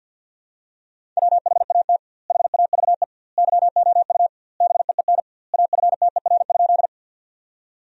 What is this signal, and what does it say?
Morse code, heard as a keyed tone, about 45 wpm, 700 Hz. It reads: WHAT HAVE YOU BEEN AFTER?
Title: 2x Word-Spacing - Sentences from Top 200 Words - 50wpm